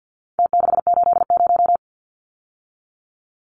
Morse code, heard as a keyed tone, about 49 wpm, 700 Hz. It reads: T680
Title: Numbers - Contest Style - 4-Digit 50wpm